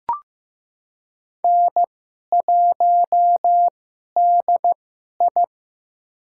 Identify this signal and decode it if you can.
Morse code, heard as a keyed tone, about 15 wpm, spaced as for 11 wpm, 700 Hz. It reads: N1DI
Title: USA Call Signs - 4 Characters - 2x Character Spacing - 15wpm